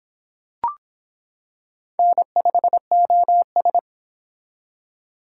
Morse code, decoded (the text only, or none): N5OS